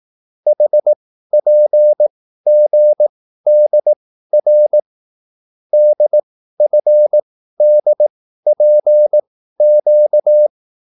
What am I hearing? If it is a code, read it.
Morse code, heard as a keyed tone, about 18 wpm, spaced as for 13 wpm, 600 Hz. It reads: HPGDR DFDPQ
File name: QSL / CW lesson